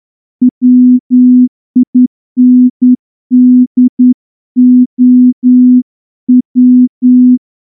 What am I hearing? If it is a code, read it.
Morse code, heard as a keyed tone, about 10 wpm, 250 Hz. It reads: WINDOW
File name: morse-training-10WPM